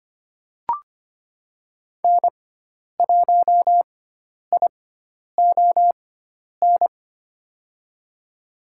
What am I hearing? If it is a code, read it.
Morse code, heard as a keyed tone, about 25 wpm, spaced as for 10 wpm, 700 Hz. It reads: N1ION
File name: Lesson 16g - Letter c - Call Signs 25wpm